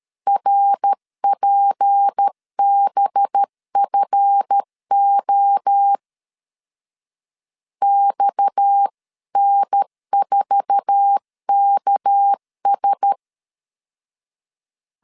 Morse code, heard as a keyed tone, about 13 wpm, 800 Hz. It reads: RPBFO XN4KS
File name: LEZIONI / 8 YJ45 / 8 YJ45 12wpm 2151